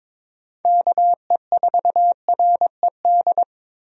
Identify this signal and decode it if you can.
Morse code, heard as a keyed tone, about 22 wpm, 700 Hz. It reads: KE4RED
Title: USA Call Signs - 6 Characters - Repeated 3x - 22wpm